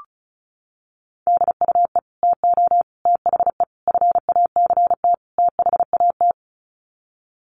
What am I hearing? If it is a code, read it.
Morse code, heard as a keyed tone, about 34 wpm, 700 Hz. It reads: DUE TO THE FACT THAT
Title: Sets of 5 Words Encore 35wpm